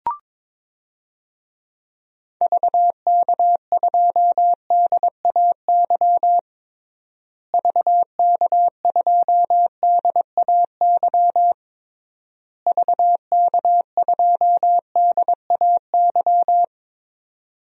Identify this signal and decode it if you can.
Morse code, heard as a keyed tone, about 22 wpm, 700 Hz. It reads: VK2DAY VK2DAY VK2DAY
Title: DXCC-100 Call Signs - 6 Characters - Repeated 3x - 22wpm